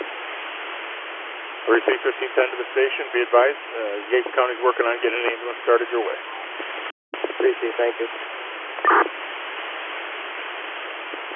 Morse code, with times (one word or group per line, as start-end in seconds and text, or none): none